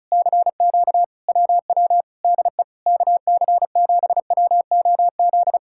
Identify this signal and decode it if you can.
Morse code, heard as a keyed tone, about 40 wpm, 700 Hz. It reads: CQ WW DE KC7WOZ